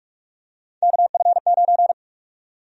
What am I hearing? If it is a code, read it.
Morse code, heard as a keyed tone, about 46 wpm, 700 Hz. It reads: KF9
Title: USA Call Sign Prefixes - 45wpm